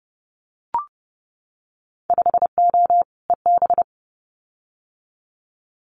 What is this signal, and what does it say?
Morse code, heard as a keyed tone, about 30 wpm, 700 Hz. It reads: 5O EB
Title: Winter Field Day Exchanges Encore 30wpm